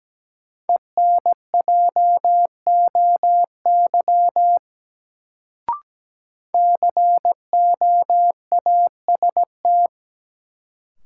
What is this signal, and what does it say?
Morse code, heard as a keyed tone, about 17 wpm, 700 Hz. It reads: ENJOY COAST
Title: Sentences from Top 1000 Words - Review Part 4 - New Word Drill - 17wpm